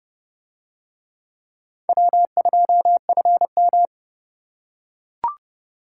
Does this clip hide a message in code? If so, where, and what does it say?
Yes: Morse code from 1.9 to 3.9 s, heard as a keyed tone, about 31 wpm, 700 Hz. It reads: W2FM